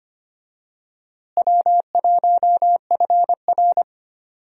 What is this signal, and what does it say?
Morse code, heard as a keyed tone, about 25 wpm, 700 Hz. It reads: W1FR
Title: USA Call Signs - 4 Characters - Repeated 3x - 25wpm